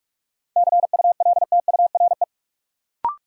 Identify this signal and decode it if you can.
Morse code, heard as a keyed tone, about 45 wpm, 700 Hz. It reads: CULTURE
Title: Sentences from Top 1000 Words - Review Part 6 - New Word Drill - 45wpm